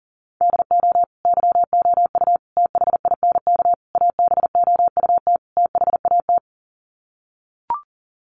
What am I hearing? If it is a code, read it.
Morse code, heard as a keyed tone, about 40 wpm, 700 Hz. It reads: DO YOU THINK ABOUT THAT
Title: Sets of 5 Words Encore2 40wpm